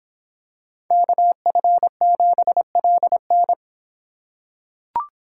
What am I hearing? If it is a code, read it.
Morse code, heard as a keyed tone, about 26 wpm, 700 Hz. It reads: KF7LN